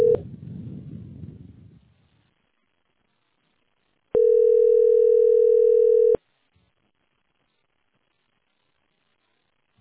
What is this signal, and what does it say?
Ringback tone (North America), 440 + 480 Hz.